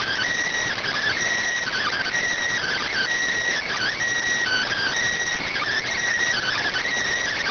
Emergency Alert System (SAME) tones (header bursts), heard as harsh buzzing bursts. It is heard from 0.4 to 6.5 s.